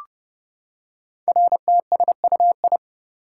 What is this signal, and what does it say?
Morse code, heard as a keyed tone, about 28 wpm, 700 Hz. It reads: RTSUI
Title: Code Groups - 5 Letters - 30wpm